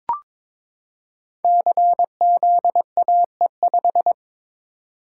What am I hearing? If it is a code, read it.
Morse code, heard as a keyed tone, about 22 wpm, 700 Hz. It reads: CZAE5
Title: Code Groups - 5 Letter:Numbers - 22wpm